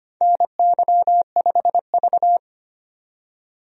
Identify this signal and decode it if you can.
Morse code, heard as a keyed tone, about 25 wpm, 700 Hz. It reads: NY5V
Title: USA Call Signs - 4 Characters - 25wpm